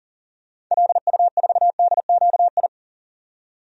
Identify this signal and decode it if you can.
Morse code, heard as a keyed tone, about 40 wpm, 700 Hz. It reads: LU4DQI
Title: DXCC-100 Call Signs - 6 Characters - 40wpm